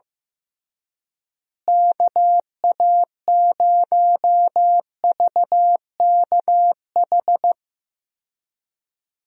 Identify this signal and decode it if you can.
Morse code, heard as a keyed tone, about 15 wpm, 700 Hz. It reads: KA0VKH
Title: USA Call Signs - 6 Characters - Repeated 3x - 15wpm